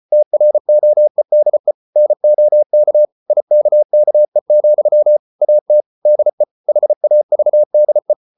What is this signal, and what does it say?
Morse code, heard as a keyed tone, about 34 wpm, 600 Hz. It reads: TROEDE NOK IKKE, AT DE HAVDE
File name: Ferie QRQ 34wpm